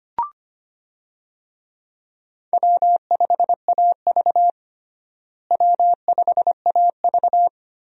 Morse code, heard as a keyed tone, about 25 wpm, 700 Hz. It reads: W5AV W5AV
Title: USA Call Signs - 4 Characters - Repeated 3x - 25wpm